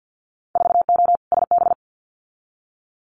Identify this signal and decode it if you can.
Morse code, heard as a keyed tone, about 50 wpm, 700 Hz. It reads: 4O SB
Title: Winter Field Day Exchanges 50wpm